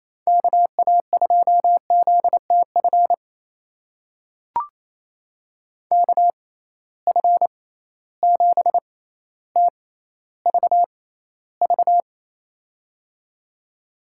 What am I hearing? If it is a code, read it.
Morse code, heard as a keyed tone, about 28 wpm, spaced as for 14 wpm, 700 Hz. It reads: KA2ZTF KF7TVV
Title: USA Call Signs - 6 Characters - 6x Character Spacing - 28wpm